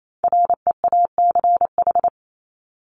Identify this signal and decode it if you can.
Morse code, heard as a keyed tone, about 28 wpm, 700 Hz. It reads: REACH
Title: Top 500 Words Repeated 28wpm